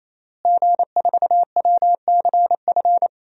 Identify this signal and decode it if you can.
Morse code, heard as a keyed tone, about 28 wpm, 700 Hz. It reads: G4WCF
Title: DXCC-100 Call Signs - 5 Characters - Repeated 3x - 28wpm